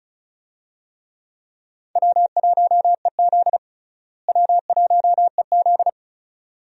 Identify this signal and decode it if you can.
Morse code, heard as a keyed tone, about 34 wpm, 700 Hz. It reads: W1EZ W1EZ
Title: USA Call Signs - 4 Characters - Repeated 2x - 35wpm